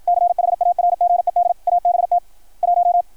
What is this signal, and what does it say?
Morse code, heard as a keyed tone, about 53 wpm, 690 Hz. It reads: CHARGED ABT 1
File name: archived / CRSnet